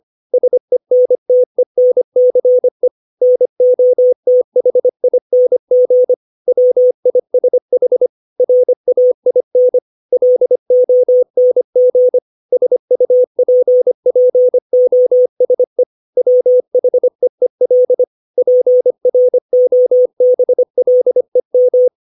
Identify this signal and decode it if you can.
Morse code, heard as a keyed tone, about 25 wpm, 500 Hz. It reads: SENTENCE NOTHING WISH RAIN LONG SUPPOSE WHEEL PROBLEM
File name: 632 words / 632 Words 25WPM